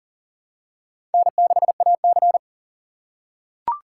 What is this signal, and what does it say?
Morse code, heard as a keyed tone, about 40 wpm, 700 Hz. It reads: N6AC
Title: USA Call Signs - 4 Characters - 2x Character Spacing - 40wpm